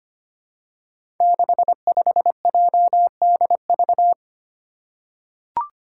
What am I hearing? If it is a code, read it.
Morse code, heard as a keyed tone, about 25 wpm, 700 Hz. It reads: 65JDV
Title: Five Challenging Letter-Number Combinations 25wpm